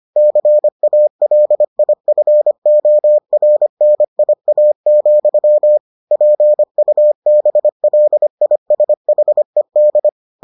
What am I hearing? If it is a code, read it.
Morse code, heard as a keyed tone, about 25 wpm, 600 Hz. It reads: CALIFORNIA, PUBLISHED